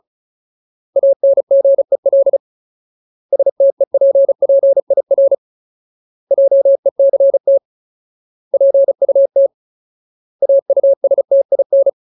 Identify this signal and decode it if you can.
Morse code, heard as a keyed tone, about 34 wpm, 550 Hz. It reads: ANGEL STEPPIR JECT PUT AUSTIN
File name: Ultimate Word List / 35WPM Ultimate